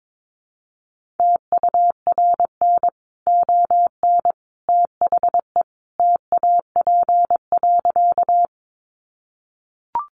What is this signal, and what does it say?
Morse code, heard as a keyed tone, about 22 wpm, 700 Hz. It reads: TURN ON THE TAP.
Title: Sentences from Top 1400 Words 22wpm